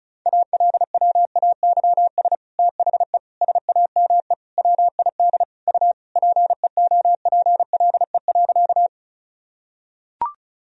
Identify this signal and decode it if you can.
Morse code, heard as a keyed tone, about 34 wpm, 700 Hz. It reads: ALWAYS THE SAME WID U PEOPLE.
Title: Sentences from Top 300 Words and Abbreviations 35wpm